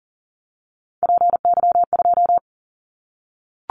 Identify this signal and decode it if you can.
Morse code, heard as a keyed tone, about 40 wpm, 700 Hz. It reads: PY2
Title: Callsign Prefixes for Top 100 DXCC - 40wpm